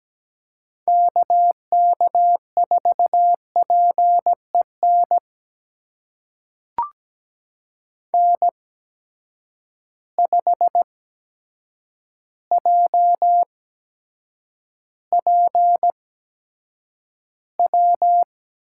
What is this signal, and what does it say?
Morse code, heard as a keyed tone, about 17 wpm, spaced as for 8 wpm, 700 Hz. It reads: KK4PEN N5JPW
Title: USA Call Signs - 8x Character Spacing - 17wpm